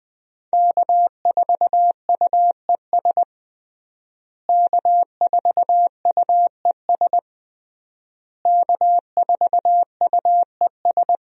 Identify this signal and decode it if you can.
Morse code, heard as a keyed tone, about 20 wpm, 700 Hz. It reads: K4UES K4UES K4UES
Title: USA Call Signs - 5 Characters - Repeated 3x - 20wpm